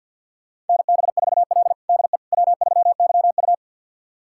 Morse code, heard as a keyed tone, about 52 wpm, 700 Hz. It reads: NB4L DE W3YU
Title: USA Callsign DE Callsign - 4 Characters - 50wpm